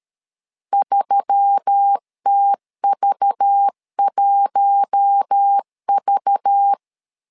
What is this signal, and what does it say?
Morse code, heard as a keyed tone, about 13 wpm, 800 Hz. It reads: 3TV1V
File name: LEZIONI / 5 LBVC / 5 LBV 12wpm 1855